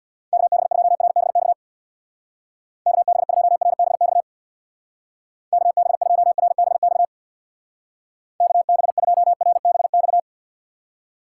Morse code, heard as a keyed tone, about 65 wpm, 700 Hz. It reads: KB2RBX KB2RBX KB2RBX KB2RBX